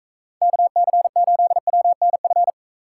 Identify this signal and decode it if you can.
Morse code, heard as a keyed tone, about 39 wpm, 700 Hz. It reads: KC8WNF